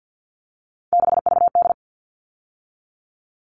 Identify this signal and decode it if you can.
Morse code, heard as a keyed tone, about 50 wpm, 700 Hz. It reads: 6VD
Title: All Cut Numbers - 3-Digit 50wpm